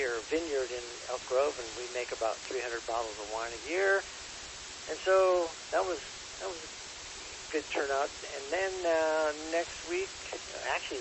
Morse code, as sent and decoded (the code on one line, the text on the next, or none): none